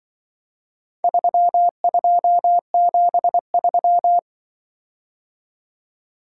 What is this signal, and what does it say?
Morse code, heard as a keyed tone, about 24 wpm, 700 Hz. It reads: 3273
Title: Speed-Racing - Four Letter-Number Combinations - 20wpm